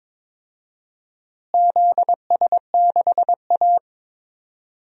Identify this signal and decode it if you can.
Morse code, heard as a keyed tone, about 22 wpm, 700 Hz. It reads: ZS6A
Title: DXCC-100 Call Signs - 4 Characters - Repeated 3x - 22wpm